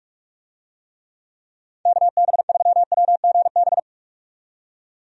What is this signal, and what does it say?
Morse code, heard as a keyed tone, about 46 wpm, 700 Hz. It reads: KB3WGB
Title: USA Call Signs - 6 Characters - 45wpm